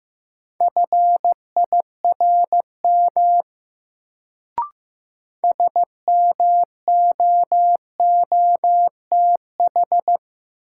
Morse code, heard as a keyed tone, about 15 wpm, 700 Hz. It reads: FIRM SMOOTH
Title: Sentences from Top 1400 Words - New Word Drill - 15wpm